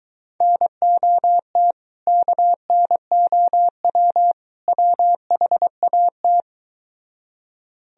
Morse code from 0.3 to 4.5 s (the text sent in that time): NOT KNOW